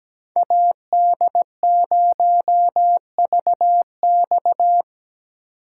Morse code, heard as a keyed tone, about 17 wpm, 700 Hz. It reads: AD0VX
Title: USA Call Signs - 5 Characters - 2x Character Spacing - 17wpm